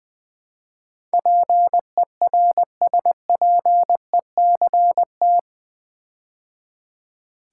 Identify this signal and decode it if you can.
Morse code, heard as a keyed tone, about 20 wpm, 700 Hz. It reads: PERSPECT
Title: Build A Word - 4 Letters At A Time - 11-Letter Words - 20wpm